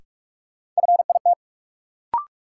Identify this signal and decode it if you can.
Morse code, heard as a keyed tone, about 45 wpm, 700 Hz. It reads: FIT
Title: Sentences from Top 800 Words - New Word Drill - 45wpm